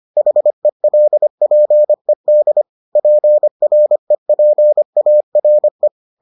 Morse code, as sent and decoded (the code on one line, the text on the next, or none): .... .-.. .--. -.. / .--. .-. .--. .- .-. .
HELPED PREPARE